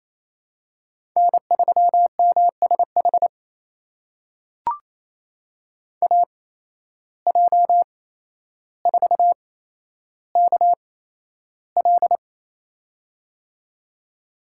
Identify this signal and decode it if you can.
Morse code, heard as a keyed tone, about 28 wpm, spaced as for 12 wpm, 700 Hz. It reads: N3MSH AJ4KL
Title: USA Call Signs - 5 Characters - 8x Character Spacing - 28wpm